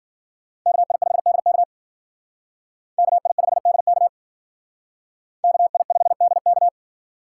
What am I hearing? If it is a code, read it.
Morse code, heard as a keyed tone, about 48 wpm, 700 Hz. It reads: KI5DK KI5DK KI5DK
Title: Speed Racing - USA Call Signs 40wpm